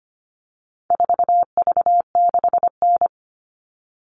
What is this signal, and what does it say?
Morse code, heard as a keyed tone, about 25 wpm, 700 Hz. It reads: 4V6N
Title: All Cut Numbers - 4-Digit 25wpm